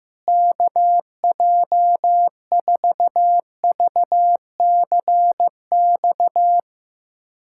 Morse code, heard as a keyed tone, about 15 wpm, 700 Hz. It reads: KJ4VCX
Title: USA Callsign DE Callsign 15wpm